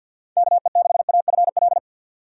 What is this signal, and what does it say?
Morse code, heard as a keyed tone, about 52 wpm, 700 Hz. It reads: KE6AFL
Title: USA Call Signs - 6 Characters - 50wpm